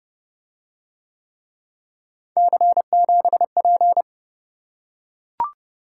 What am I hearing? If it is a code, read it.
Morse code, heard as a keyed tone, about 30 wpm, 700 Hz. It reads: C7P